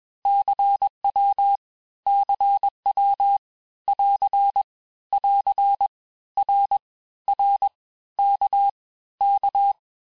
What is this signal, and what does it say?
Morse code, heard as a keyed tone, about 21 wpm, 800 Hz. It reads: CW CW <AR> <AR> R R K K
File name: cours-du-lundi files / Cours-1586-texte-abreviations-vitesse-15mpm-puis-20mpm